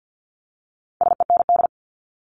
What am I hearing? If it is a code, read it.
Morse code, heard as a keyed tone, about 50 wpm, 700 Hz. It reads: SEND
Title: Top 1251-1500 Words 50wpm